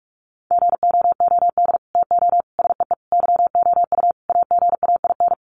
Morse code, heard as a keyed tone, about 44 wpm, 700 Hz. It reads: GOOD TO SEE YOU AGAIN
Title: Sets of 5 Words Encore2 45wpm